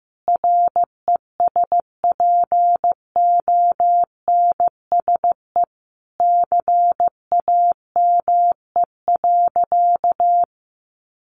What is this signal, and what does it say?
Morse code, heard as a keyed tone, about 15 wpm, 700 Hz. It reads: RESPONSE CAME.
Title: Sentences from Top 1400 Words 15wpm